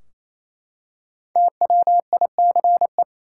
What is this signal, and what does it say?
Morse code, heard as a keyed tone, about 28 wpm, 700 Hz. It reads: TWICE
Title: Five Letter Words - 28wpm